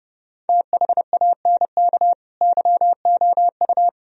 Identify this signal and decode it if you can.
Morse code, heard as a keyed tone, about 28 wpm, 700 Hz. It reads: THANK YOU